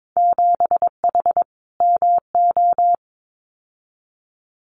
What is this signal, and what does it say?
Morse code, heard as a keyed tone, about 22 wpm, 700 Hz. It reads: 7H MO
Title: Winter Field Day Exchanges Encore 22wpm